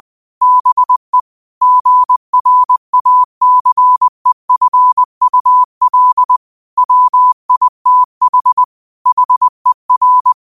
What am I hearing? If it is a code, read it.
Morse code, heard as a keyed tone, about 20 wpm, 1000 Hz. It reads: BE GRACEFUL WITH HER
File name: qotdfiles / 20wpm